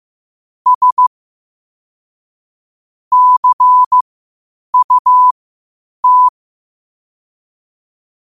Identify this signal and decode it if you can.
Morse code, heard as a keyed tone, about 15 wpm, spaced as for 8 wpm, 1000 Hz. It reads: S CUT
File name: qotdfiles / 8wpm